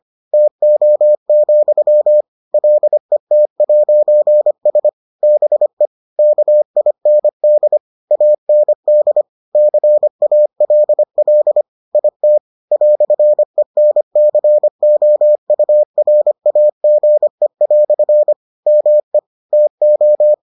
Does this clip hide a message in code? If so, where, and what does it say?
Yes: Morse code from 0.3 to 20.3 s, heard as a keyed tone, about 25 wpm, 600 Hz. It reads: TO, LET'S BE KIND AND CALL IT "ENCOURAGE" ME TO